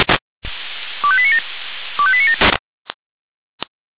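Five-tone selective call (ZVEI), heard as quick runs of five tones, tones of about 70 ms. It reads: 25897 25897
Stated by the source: ZVEI five-tone call received by SDR.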